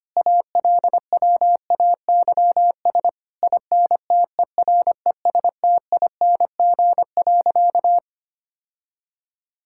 Morse code, heard as a keyed tone, about 25 wpm, 700 Hz. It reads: ALWAYS INTERESTING.
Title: Sentences from Top 1000 Words - Review Part 5 - 25wpm